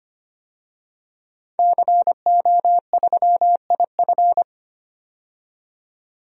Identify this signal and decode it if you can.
Morse code, heard as a keyed tone, about 25 wpm, 690 Hz. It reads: CO3IF
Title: DXCC-100 Call Signs - 5 Characters - 25wpm